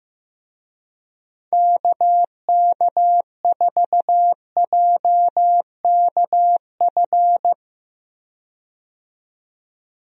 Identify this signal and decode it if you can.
Morse code, heard as a keyed tone, about 15 wpm, 700 Hz. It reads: KK4JKF
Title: USA Call Signs - 6 Characters - 15wpm